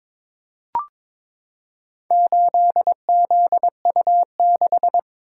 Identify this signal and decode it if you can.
Morse code, heard as a keyed tone, about 21 wpm, 700 Hz. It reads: 8ZU6
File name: practice / 22wpm / Four Letter-Number Combinations - 22wpm